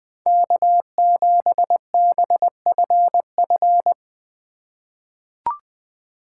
Morse code, heard as a keyed tone, about 20 wpm, 700 Hz. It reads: K7BFF
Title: USA Call Signs - 5 Characters - 2x Character Spacing - 20wpm